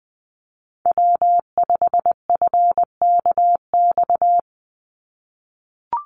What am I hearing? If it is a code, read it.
Morse code, heard as a keyed tone, about 20 wpm, 700 Hz. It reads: W5FKX
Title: Call Signs - 20wpm